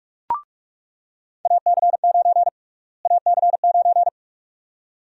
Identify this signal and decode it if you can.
Morse code, heard as a keyed tone, about 46 wpm, 700 Hz. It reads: AC9 AC9
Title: USA Call Sign Prefixes - Repeated 2x - 45wpm